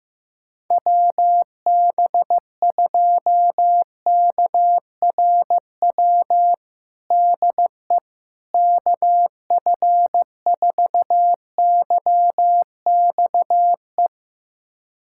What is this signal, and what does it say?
Morse code, heard as a keyed tone, about 15 wpm, 700 Hz. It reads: WB2KRW DE KF4YXE
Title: USA Callsign DE Callsign - 6 Characters - 15wpm